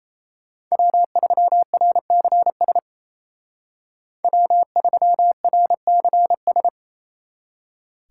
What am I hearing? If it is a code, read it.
Morse code, heard as a keyed tone, about 34 wpm, 700 Hz. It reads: W3RCS W3RCS
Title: Speed Racing - USA 1x3 Call Signs 28wpm